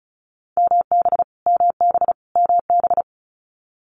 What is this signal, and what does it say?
Morse code, heard as a keyed tone, about 36 wpm, 700 Hz. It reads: MB MB MB
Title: Multi-letter Phonemes Repeated 3x 35wpm